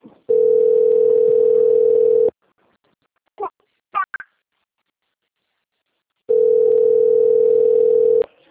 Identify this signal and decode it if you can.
Ringback tone (North America), 440 + 480 Hz.
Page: TLMK - Rio de janeiro 2